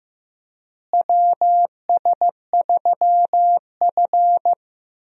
Morse code, heard as a keyed tone, about 15 wpm, 700 Hz. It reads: WS3F
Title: USA Call Signs - 4 Characters - Repeated 2x - 15wpm